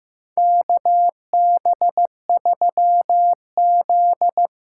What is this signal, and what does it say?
Morse code, heard as a keyed tone, about 15 wpm, 700 Hz. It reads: KB3Z
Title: USA Call Signs - 6 Characters - 6x Character Spacing - 15wpm